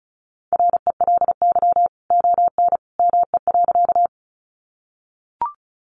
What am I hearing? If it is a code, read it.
Morse code, heard as a keyed tone, about 34 wpm, 700 Hz. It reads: RELY ON ME.
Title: Sentences from Top 1200 Words 35wpm